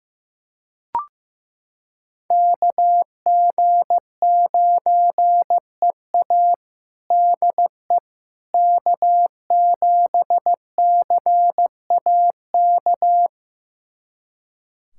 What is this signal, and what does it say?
Morse code, heard as a keyed tone, about 15 wpm, 700 Hz. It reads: KG9EA DE K7CAK
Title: USA Callsign DE Callsign - 5 Characters - 15wpm